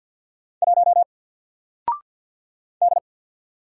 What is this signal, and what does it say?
Morse code, heard as a keyed tone, about 48 wpm, 700 Hz. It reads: 1 D
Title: Single Character (Letter:Number:Punctuation:Prosign) 50wpm